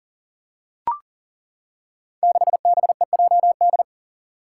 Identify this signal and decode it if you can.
Morse code, heard as a keyed tone, about 42 wpm, 700 Hz. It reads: 6BEJD
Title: Five Challenging Letter-Number Combinations 40wpm